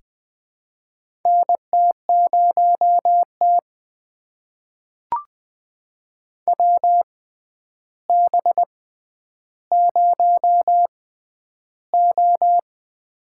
Morse code, heard as a keyed tone, about 21 wpm, spaced as for 10 wpm, 700 Hz. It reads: NT0T WB0O
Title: USA Call Signs - 4 Characters - 6x Character Spacing - 20wpm